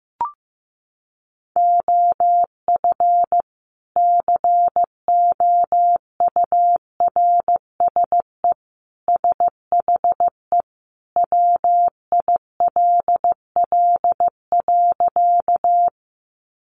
Morse code, heard as a keyed tone, about 15 wpm, 700 Hz. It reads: OF COURSE SHE WILL.
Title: Sentences from Top 500 Words 15wpm